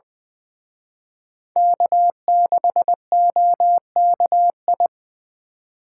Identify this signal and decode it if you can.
Morse code, heard as a keyed tone, about 25 wpm, 700 Hz. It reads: K6OKI
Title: Speed Racing - USA 1x3 Call Signs 20wpm